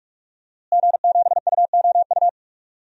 Morse code, heard as a keyed tone, about 46 wpm, 700 Hz. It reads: G7UOU
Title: DXCC-100 Call Signs - 5 Characters - Repeated 3x - 45wpm